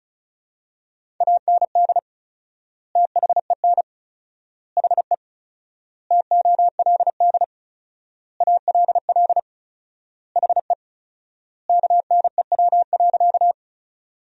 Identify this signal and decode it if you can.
Morse code, heard as a keyed tone, about 34 wpm, 700 Hz. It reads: AND THEN HE TOLD ALL HE KNEW.
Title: 4x Word-Spacing - Sentences from Top 400 Words - 35wpm